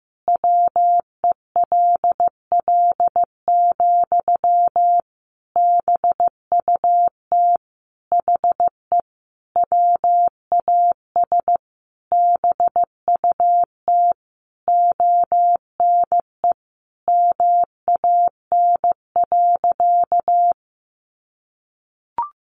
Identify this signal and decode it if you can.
Morse code, heard as a keyed tone, about 15 wpm, 700 Hz. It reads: WELL, BUT HE WAS BUT ONE MAN.
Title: Sentences from Top 1500 Words 15wpm